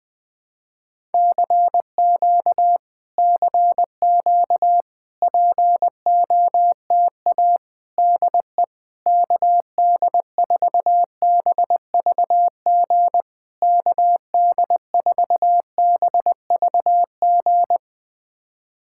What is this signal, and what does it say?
Morse code, heard as a keyed tone, about 21 wpm, 700 Hz. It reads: CQ CQ POTA DE KD4BVG KD4BVG
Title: POTA QSO Practice - 20wpm